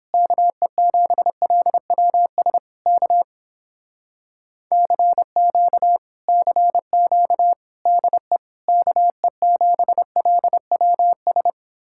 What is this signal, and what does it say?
Morse code, heard as a keyed tone, about 29 wpm, 700 Hz. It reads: KE7LWS K CQ CQ DE KE7LWS